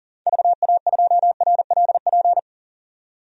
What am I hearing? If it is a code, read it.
Morse code, heard as a keyed tone, about 40 wpm, 700 Hz. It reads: VA2RLP
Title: DXCC-100 Call Signs - 6 Characters - 40wpm